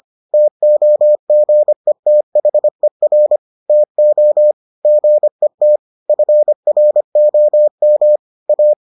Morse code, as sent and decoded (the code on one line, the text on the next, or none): - --- --. - .... .-. / - --- / --. - / ..-. .-. --- -- / .-
TOGETHER TO GET FROM A